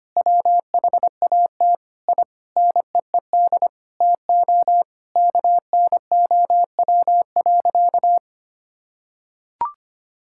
Morse code, heard as a keyed tone, about 25 wpm, 700 Hz. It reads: WHAT I NEED TO KNOW.
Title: Sentences from Top 1000 Words - Review Part 5 - 25wpm